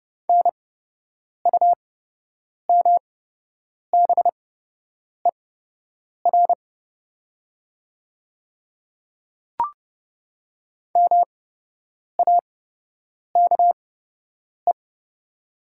Morse code, heard as a keyed tone, about 30 wpm, spaced as for 8 wpm, 700 Hz. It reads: NUMBER MAKE
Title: 8x Character Spacing - Top 300 Words - 30wpm